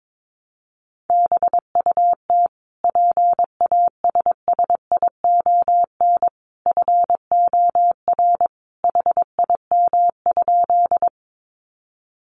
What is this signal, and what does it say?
Morse code, heard as a keyed tone, about 22 wpm, 700 Hz. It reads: BUT PASSION FOR HIM?
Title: Sentences from Top 1300 Words 22wpm